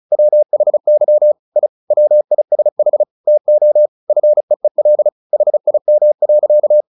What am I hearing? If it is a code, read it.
Morse code, heard as a keyed tone, about 34 wpm, 600 Hz. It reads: WHY I WISH TO FEEL HIM.
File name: the old man and the sea 2 35wpm